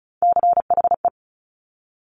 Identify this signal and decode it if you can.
Morse code, heard as a keyed tone, about 36 wpm, 700 Hz. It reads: CHE
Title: Top Three Letter Combinations in Words 35wpm